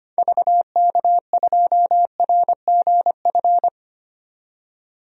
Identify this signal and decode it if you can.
Morse code, heard as a keyed tone, about 25 wpm, 690 Hz. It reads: VK2RGF
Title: DXCC-100 Call Signs - 6 Characters - 25wpm